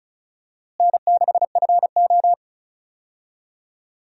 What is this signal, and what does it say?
Morse code, heard as a keyed tone, about 36 wpm, 700 Hz. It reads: N6FO
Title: USA Call Signs - 4 Characters - Repeated 2x - 35wpm